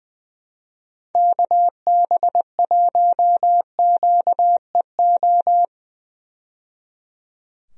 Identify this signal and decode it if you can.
Morse code, heard as a keyed tone, about 20 wpm, 700 Hz. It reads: KB1QEO
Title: USA Call Signs - 6 Characters - 20wpm